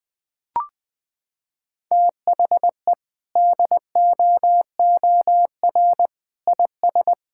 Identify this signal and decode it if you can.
Morse code, heard as a keyed tone, about 20 wpm, 700 Hz. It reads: THE DOOR IS
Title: Sentences from Top 1600 Words 20wpm